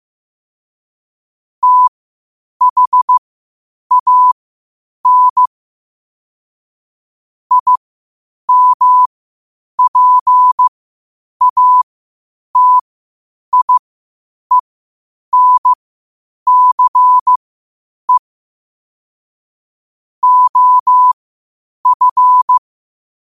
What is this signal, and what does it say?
Morse code, heard as a keyed tone, about 15 wpm, spaced as for 8 wpm, 1000 Hz. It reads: THAN IMPATIENCE OF
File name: qotdfiles / 8wpm